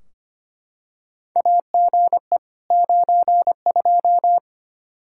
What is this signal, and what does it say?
Morse code, heard as a keyed tone, about 25 wpm, 700 Hz. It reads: AGE 92
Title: Practice QSO - Exchange 3 - More Info - 25wpm